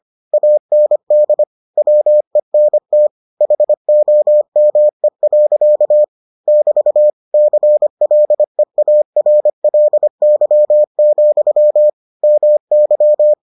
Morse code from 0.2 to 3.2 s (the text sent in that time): AND WENT